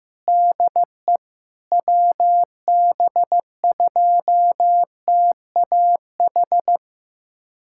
Morse code, heard as a keyed tone, about 16 wpm, 700 Hz. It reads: DE WB2TAH
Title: USA Callsign DE Callsign - 6 Characters - 15wpm